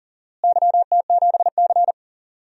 Morse code, heard as a keyed tone, about 40 wpm, 700 Hz. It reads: YT7C
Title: DXCC-100 Call Signs - 4 Characters - Repeated 3x - 40wpm